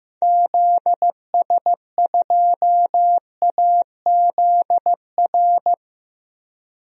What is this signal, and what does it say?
Morse code, heard as a keyed tone, about 15 wpm, 700 Hz. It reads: ZS2AZR
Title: DXCC-100 Call Signs - 6 Characters - 15wpm